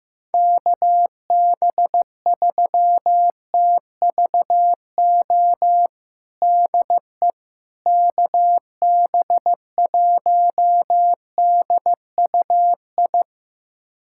Morse code, heard as a keyed tone, about 15 wpm, 700 Hz. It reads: KB3TVO DE KB1DUI
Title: USA Callsign DE Callsign - 6 Characters - 15wpm